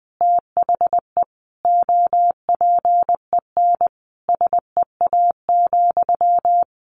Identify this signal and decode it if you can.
Morse code, heard as a keyed tone, about 20 wpm, 700 Hz. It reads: THE OPEN SEA,
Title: Sentences from Top 1200 Words 20wpm